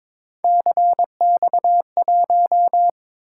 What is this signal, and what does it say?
Morse code, heard as a keyed tone, about 22 wpm, 700 Hz. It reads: CX1
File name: practice / 22wpm / Three Letter-Number Combinations - 22wpm